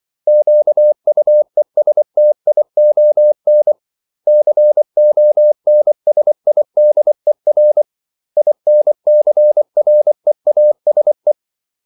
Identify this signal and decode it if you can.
Morse code, heard as a keyed tone, about 24 wpm, 600 Hz. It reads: QUESTION CONSIDER INCREASE
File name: CW / Practice / 500MostCommonEnglishWords24